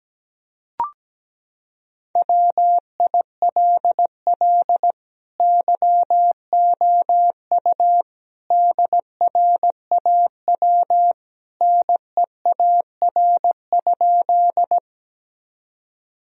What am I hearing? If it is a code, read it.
Morse code, heard as a keyed tone, about 17 wpm, 700 Hz. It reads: WILL YOU DRAW NEAR?
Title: Sentences from Top 1000 Words - Review Part 1 - 17wpm